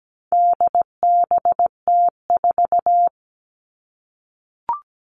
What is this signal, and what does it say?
Morse code, heard as a keyed tone, about 17 wpm, 700 Hz. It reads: DBT4
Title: All Cut Numbers - 4-Digit 17wpm